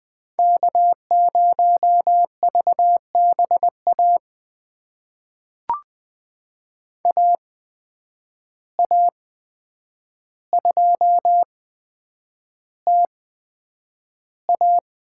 Morse code, heard as a keyed tone, about 20 wpm, spaced as for 9 wpm, 700 Hz. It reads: K0VBA AA2TA
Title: USA Call Signs - 5 Characters - 8x Character Spacing - 20wpm